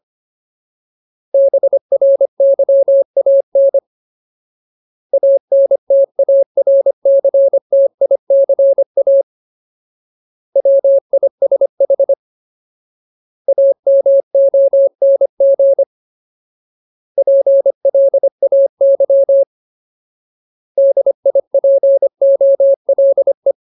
Morse code, heard as a keyed tone, about 25 wpm, 550 Hz. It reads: BRYAN ANTARCTICA WISH AMONG PLAY DIPOLE
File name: Ultimate Word List / 25WPM Ultimate